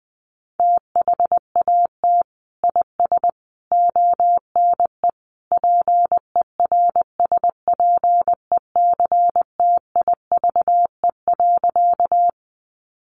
Morse code, heard as a keyed tone, about 20 wpm, 700 Hz. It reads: THAT IS ONE PERSPECTIVE.
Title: Sentences from Top 1200 Words 20wpm